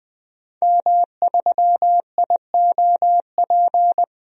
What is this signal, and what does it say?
Morse code, heard as a keyed tone, about 20 wpm, 700 Hz. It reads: M3IOP
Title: DXCC-100 Call Signs - 5 Characters - 20wpm